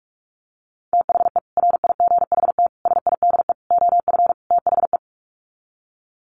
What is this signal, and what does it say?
Morse code, heard as a keyed tone, about 44 wpm, 700 Hz. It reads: THE RIGHT SIDE OF THE
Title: Sets of 5 Words Encore 45wpm